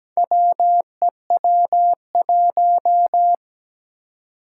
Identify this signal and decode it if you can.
Morse code, heard as a keyed tone, about 17 wpm, 700 Hz. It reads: WEW1
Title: Four Challenging Letter-Number Combinations 17wpm